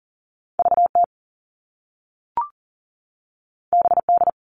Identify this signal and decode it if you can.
Morse code, heard as a keyed tone, about 42 wpm, 700 Hz. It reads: VT BD
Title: All Cut Numbers - 2-Digit 40wpm